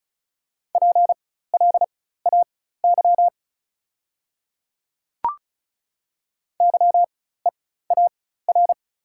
Morse code, heard as a keyed tone, about 36 wpm, spaced as for 16 wpm, 700 Hz. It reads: PLAY YEAR
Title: 4x Character Spacing - Top 200 Words - 35wpm